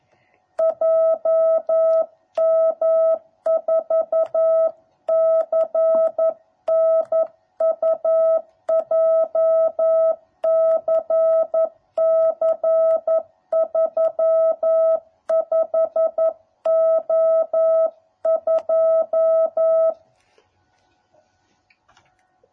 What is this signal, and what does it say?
Morse code, heard as a keyed tone, about 11 wpm, 650 Hz. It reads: JM4CNU JCC 35O2